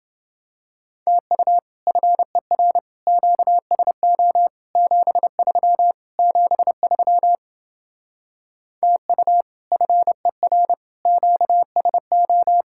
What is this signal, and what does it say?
Morse code, heard as a keyed tone, about 29 wpm, 700 Hz. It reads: TU FER QSO 73 73 TU FER QSO